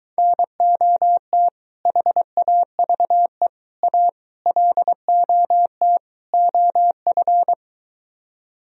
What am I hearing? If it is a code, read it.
Morse code, heard as a keyed tone, about 22 wpm, 700 Hz. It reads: NOT HAVE A LOT OF